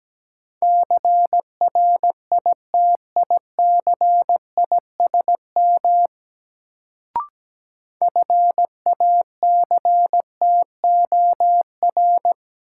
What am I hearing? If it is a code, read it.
Morse code, heard as a keyed tone, about 17 wpm, 700 Hz. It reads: CRITICISM FACTOR
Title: Sentences from Top 1300 Words - New Word Drill Alt - 17wpm